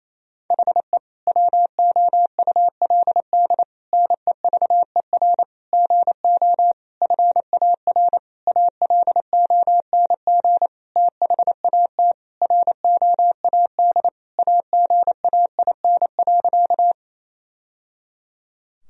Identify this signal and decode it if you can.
Morse code, heard as a keyed tone, about 28 wpm, 700 Hz. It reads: HE WOULD NEVER GO FAR ALONG THAT ROAD AGAIN.
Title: Sentences from Top 1000 Words - Review Part 2 - 28wpm